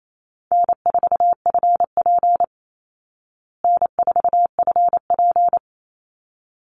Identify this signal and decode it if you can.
Morse code, heard as a keyed tone, about 28 wpm, 700 Hz. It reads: N4FP N4FP
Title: Easy Call Signs Repeated 2x - 28wpm